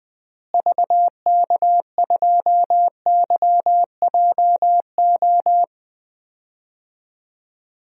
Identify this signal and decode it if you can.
Morse code, heard as a keyed tone, about 20 wpm, 700 Hz. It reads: VK2YJO
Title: DXCC-100 Call Signs - 6 Characters - Repeated 2x - 20wpm